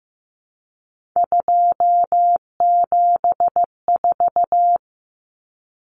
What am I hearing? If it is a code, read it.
Morse code, heard as a keyed tone, about 14 wpm, 700 Hz. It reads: 274
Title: Numbers - 3-Digit 15wpm